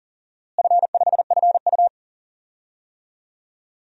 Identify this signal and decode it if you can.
Morse code, heard as a keyed tone, about 39 wpm, 700 Hz. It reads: F5FU
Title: DXCC-100 Call Signs - 4 Characters - Repeated 3x - 40wpm